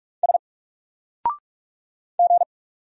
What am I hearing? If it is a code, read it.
Morse code, heard as a keyed tone, about 44 wpm, 700 Hz. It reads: S G
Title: Single Character (Letter:Number:Punctuation:Prosign) Alt 45wpm